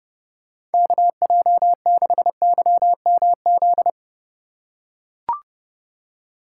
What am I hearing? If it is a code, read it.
Morse code, heard as a keyed tone, about 31 wpm, 700 Hz. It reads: KJ6YMZ
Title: USA Call Signs - 6 Characters - 4x Character Spacing - 30wpm